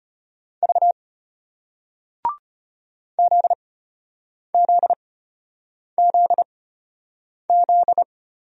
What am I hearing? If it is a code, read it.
Morse code, heard as a keyed tone, about 34 wpm, spaced as for 8 wpm, 700 Hz. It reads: V ZZZZ